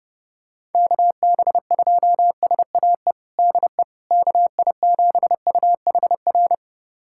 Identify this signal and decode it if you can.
Morse code, heard as a keyed tone, about 32 wpm, 700 Hz. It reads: KB2SAE DE KI7UHR
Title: USA Callsign DE Callsign - 6 Characters - 30wpm